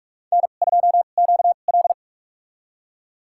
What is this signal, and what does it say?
Morse code, heard as a keyed tone, about 46 wpm, spaced as for 32 wpm, 700 Hz. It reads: N2QL
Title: USA Call Signs - 4 Characters - 2x Character Spacing - 45wpm